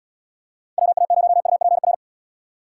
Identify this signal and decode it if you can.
Morse code, heard as a keyed tone, about 77 wpm, 700 Hz. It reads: WA0UPR